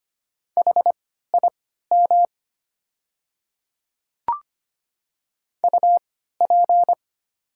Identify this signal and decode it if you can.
Morse code, heard as a keyed tone, about 25 wpm, spaced as for 14 wpm, 690 Hz. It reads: HIM UP
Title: 3x Character Spacing - Top 200 Words - 25wpm